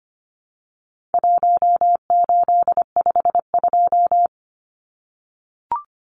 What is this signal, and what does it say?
Morse code, heard as a keyed tone, about 25 wpm, 700 Hz. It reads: 1852
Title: Numbers - 4-Digit 25wpm